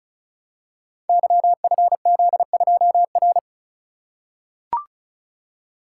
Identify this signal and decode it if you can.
Morse code, heard as a keyed tone, about 36 wpm, 700 Hz. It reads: YFZ2R
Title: Code Groups - 5 Letter:Numbers - 35wpm